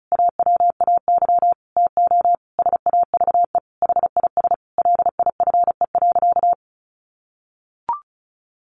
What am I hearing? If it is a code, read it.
Morse code, heard as a keyed tone, about 34 wpm, 700 Hz. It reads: AWAY TO SAVE HIS LIFE.
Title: Sentences from Top 1100 Words 35wpm